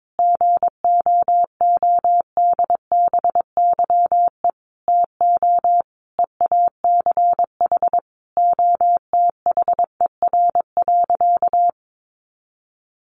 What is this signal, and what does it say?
Morse code, heard as a keyed tone, about 22 wpm, 700 Hz. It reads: GOODBYE TO EACH OTHER.
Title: Additional Easy Phrases 22wpm